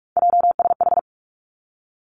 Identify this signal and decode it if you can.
Morse code, heard as a keyed tone, about 46 wpm, 700 Hz. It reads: JSH
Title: Three Challenging Letter-Number Combinations 45wpm